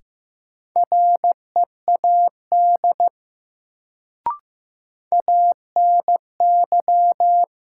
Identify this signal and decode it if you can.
Morse code, heard as a keyed tone, about 15 wpm, 700 Hz. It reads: READ ANY
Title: Sentences from Top 200 Words - New Word Drill - 15wpm